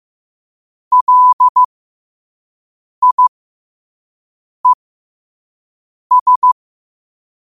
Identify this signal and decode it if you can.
Morse code, heard as a keyed tone, about 15 wpm, spaced as for 5 wpm, 1000 Hz. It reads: LIES
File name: qotdfiles / 5wpm